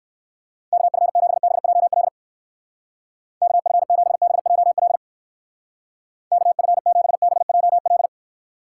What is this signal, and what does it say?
Morse code, heard as a keyed tone, about 62 wpm, 700 Hz. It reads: KF7BJL KF7BJL KF7BJL